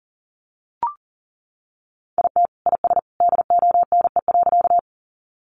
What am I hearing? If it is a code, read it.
Morse code, heard as a keyed tone, about 40 wpm, 700 Hz. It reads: IT IS DONE.
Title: Sentences from Top 500 Words 40wpm